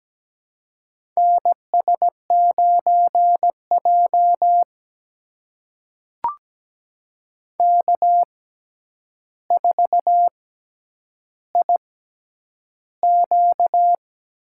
Morse code, heard as a keyed tone, about 17 wpm, spaced as for 9 wpm, 700 Hz. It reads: NS9J K4IQ
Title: USA Call Signs - 6x Character Spacing - 17wpm